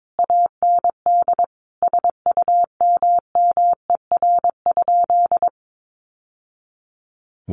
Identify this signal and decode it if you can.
Morse code, heard as a keyed tone, about 22 wpm, 700 Hz. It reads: AND SUMMER?
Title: Sentences from Top 1200 Words 22wpm